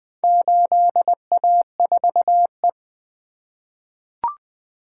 Morse code, heard as a keyed tone, about 20 wpm, 700 Hz. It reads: 8A4E
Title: Four Letter-Number Combinations - 20wpm